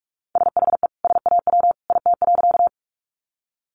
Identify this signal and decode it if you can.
Morse code, heard as a keyed tone, about 44 wpm, 700 Hz. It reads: SHE SAW IT.
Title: Sentences from Top 300 Words 45wpm